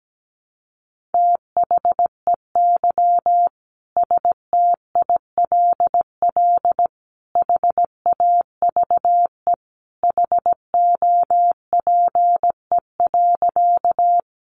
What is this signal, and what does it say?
Morse code, heard as a keyed tone, about 17 wpm, 700 Hz. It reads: THEY STILL HAVE HOPE.
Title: Sentences from Top 1000 Words 17wpm